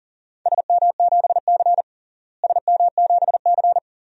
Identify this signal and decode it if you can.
Morse code, heard as a keyed tone, about 40 wpm, 700 Hz. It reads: SM7C SM7C
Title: DXCC-100 Call Signs - 4 Characters - Repeated 3x - 40wpm